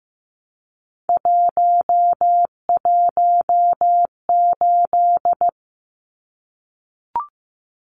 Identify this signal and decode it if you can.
Morse code, heard as a keyed tone, about 15 wpm, 700 Hz. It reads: 118
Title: Numbers - 3-Digit 15wpm